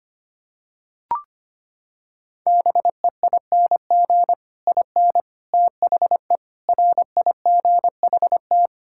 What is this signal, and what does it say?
Morse code, heard as a keyed tone, about 25 wpm, 700 Hz. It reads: BEING IN THE RIGHT
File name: Sets of 9 Words - 25wpm